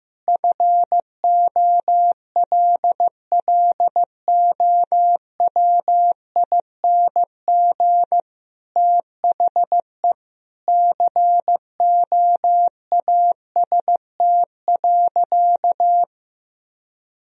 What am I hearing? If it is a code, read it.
Morse code, heard as a keyed tone, about 15 wpm, 700 Hz. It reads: FOLLOWING THE COAST.
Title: Sentences from Top 1000 Words - Review Part 4 - 15wpm